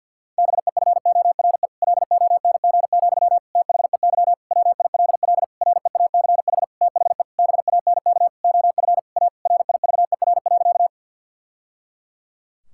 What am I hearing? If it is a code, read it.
Morse code, heard as a keyed tone, about 50 wpm, 700 Hz. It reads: BEFORE LONG, THEY WILL REACH THE BANK OF A RIVER.
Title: Sentences from Top 1000 Words - Review Part 6 - 50wpm